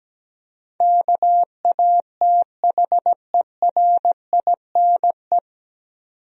Morse code, heard as a keyed tone, about 17 wpm, 700 Hz. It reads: KATHERINE
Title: USA Common Names - 17wpm